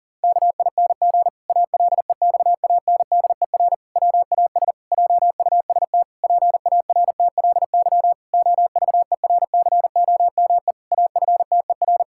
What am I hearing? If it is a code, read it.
Morse code, heard as a keyed tone, about 40 wpm, 700 Hz. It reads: KING ALEXANDER WAS JUST PARTLY OVERCOME AFTER
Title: Pangram Challenge v2 - 40wpm